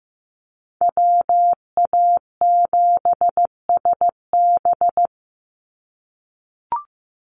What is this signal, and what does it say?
Morse code, heard as a keyed tone, about 15 wpm, 700 Hz. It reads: WA7SB
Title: NA SOTA Chaser Callsigns 15wpm